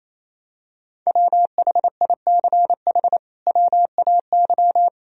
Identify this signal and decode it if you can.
Morse code, heard as a keyed tone, about 27 wpm, 700 Hz. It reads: WHICH WAY
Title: Sentences from Top 1000 Words - Review Part 2 - 28wpm